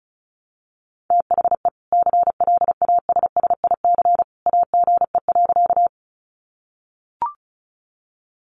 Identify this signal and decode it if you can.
Morse code, heard as a keyed tone, about 34 wpm, 700 Hz. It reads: THE CLASSIC AGE.
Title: Sentences from Top 1000 Words 35wpm